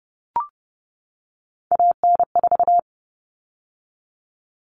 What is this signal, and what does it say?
Morse code, heard as a keyed tone, about 30 wpm, 700 Hz. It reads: AN4
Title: All Cut Numbers - 3-Digit 30wpm